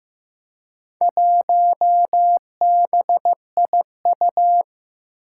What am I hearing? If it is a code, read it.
Morse code, heard as a keyed tone, about 15 wpm, 700 Hz. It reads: 1BIU
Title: Four Challenging Letter-Number Combinations 15wpm